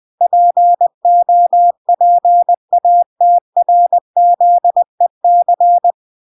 Morse code, heard as a keyed tone, about 20 wpm, 700 Hz. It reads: POPATRZEC